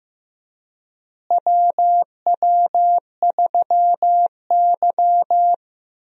Morse code, heard as a keyed tone, about 15 wpm, 700 Hz. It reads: WW3Y
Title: USA Call Signs - 4 Characters - Repeated 3x - 15wpm